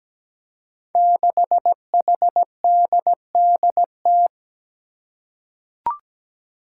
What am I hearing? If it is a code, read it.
Morse code, heard as a keyed tone, about 17 wpm, 700 Hz. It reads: 6HDDT
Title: Five Challenging Letter-Number Combinations 17wpm